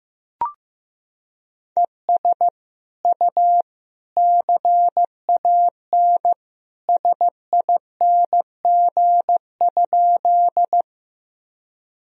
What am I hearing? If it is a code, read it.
Morse code, heard as a keyed tone, about 15 wpm, 700 Hz. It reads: ES U CAN SING?
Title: Sentences from Top 400 Words and Abbreviations 15wpm